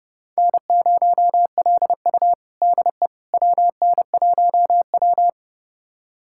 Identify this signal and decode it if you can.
Morse code, heard as a keyed tone, about 32 wpm, 700 Hz. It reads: N0LU DE WN1W
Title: USA Callsign DE Callsign - 4 Characters - 30wpm